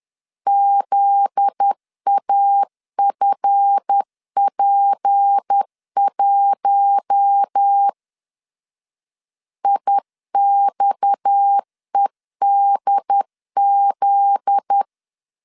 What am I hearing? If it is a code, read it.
Morse code, heard as a keyed tone, about 10 wpm, 800 Hz. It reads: ZAFP1 IXEDZ
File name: LEZIONI / 7 GF9Z / 7 GF9Z 10wpm 2611